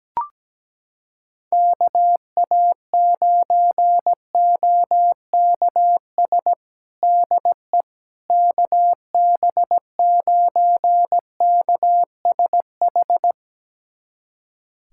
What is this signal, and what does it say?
Morse code, heard as a keyed tone, about 17 wpm, 700 Hz. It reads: KA9OKS DE KB9KSH
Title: USA Callsign DE Callsign - 6 Characters - 17wpm